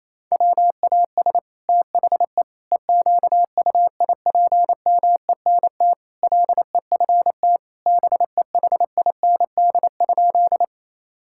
Morse code, heard as a keyed tone, about 28 wpm, 700 Hz. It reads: WAS THE EQUIPMENT LEFT BEHIND?
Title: Sentences from Top 1000 Words - Review Part 5 - 28wpm